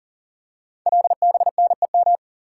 Morse code, heard as a keyed tone, about 38 wpm, 700 Hz. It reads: LBNEM
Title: Code Groups - 5 Letters - 40wpm